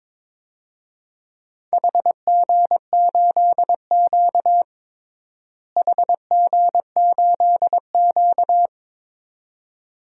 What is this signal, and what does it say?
Morse code, heard as a keyed tone, about 22 wpm, 700 Hz. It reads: HG8Q HG8Q
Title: DXCC-100 Call Signs - 4 Characters - Repeated 2x - 22wpm